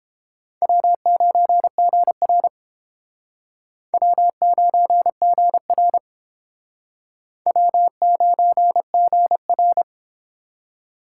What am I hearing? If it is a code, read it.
Morse code, heard as a keyed tone, about 29 wpm, 700 Hz. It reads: W9GR W9GR W9GR